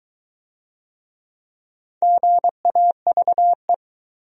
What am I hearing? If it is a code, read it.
Morse code, heard as a keyed tone, about 21 wpm, 700 Hz. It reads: GAVE